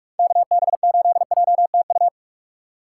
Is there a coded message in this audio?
Yes: Morse code from 0.2 to 2.1 s, heard as a keyed tone, about 44 wpm, 700 Hz. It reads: KB8JTU